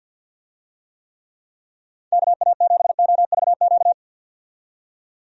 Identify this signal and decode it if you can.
Morse code, heard as a keyed tone, about 52 wpm, 700 Hz. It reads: KA7OVQ
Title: USA Call Signs - 6 Characters - 50wpm